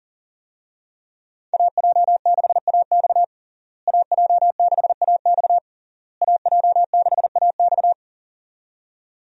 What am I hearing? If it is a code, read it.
Morse code, heard as a keyed tone, about 40 wpm, 700 Hz. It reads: AJ6AX AJ6AX AJ6AX
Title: USA Call Signs - 5 Characters - Repeated 3x - 40wpm